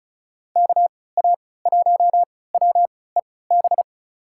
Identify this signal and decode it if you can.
Morse code, heard as a keyed tone, about 34 wpm, spaced as for 20 wpm, 700 Hz. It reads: KA1WEB
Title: USA Call Signs - 6 Characters - 3x Character Spacing - 35wpm